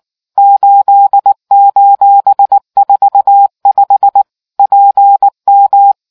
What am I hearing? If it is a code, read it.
Morse code, heard as a keyed tone, about 19 wpm, 800 Hz. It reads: 8:45 PM